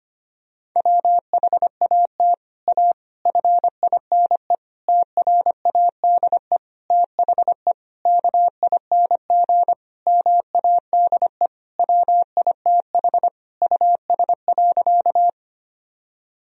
Morse code, heard as a keyed tone, about 25 wpm, 700 Hz. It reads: WHAT A FINE TRADE THE KING MADE WITH US.
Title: Sentences from Top 1000 Words - Review Part 4 - 25wpm